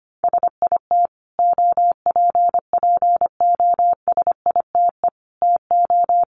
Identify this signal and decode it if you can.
Morse code, heard as a keyed tone, about 25 wpm, 700 Hz. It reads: SIT OPPOSITE TO
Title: Sentences from Top 1300 Words 25wpm